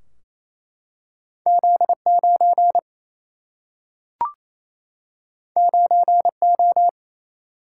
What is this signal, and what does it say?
Morse code, heard as a keyed tone, about 28 wpm, 700 Hz. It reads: Z 9 9 O
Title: Single-Letter Single-Number 28wpm